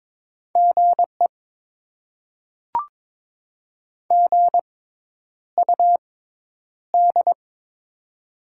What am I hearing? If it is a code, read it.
Morse code, heard as a keyed tone, about 22 wpm, spaced as for 10 wpm, 700 Hz. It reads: GE GUD
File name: Top 20 QSO Elements - 6x Character Spacing - 22wpm